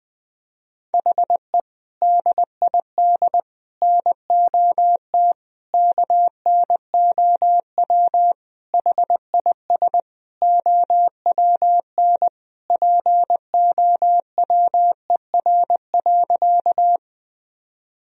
Morse code, heard as a keyed tone, about 20 wpm, 700 Hz. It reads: HE DID NOT KNOW HIS OWN POWER.
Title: Sentences from Top 1000 Words - Review Part 2 - 20wpm